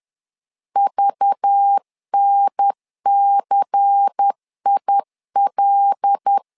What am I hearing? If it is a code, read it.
Morse code, heard as a keyed tone, about 10 wpm, 800 Hz. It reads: VNCIL
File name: LEZIONI / 5 LBVC / 5 LBVC 10wpm 2349